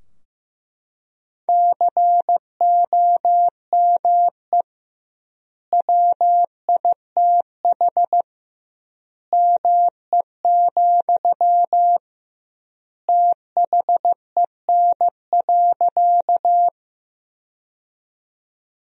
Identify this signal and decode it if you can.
Morse code, heard as a keyed tone, about 15 wpm, 700 Hz. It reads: COME WITH ME, THEN.
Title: 2x Word-Spacing - Sentences from Top 200 Words - 15wpm